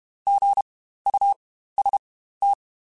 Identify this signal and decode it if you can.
Morse code, heard as a keyed tone, about 32 wpm, spaced as for 15 wpm, 800 Hz. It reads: GUST
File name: Spaced Alphas / Spaced Alphas mp3s / Gs sp @30